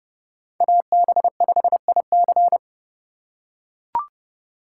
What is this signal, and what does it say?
Morse code, heard as a keyed tone, about 29 wpm, 700 Hz. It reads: AB5IC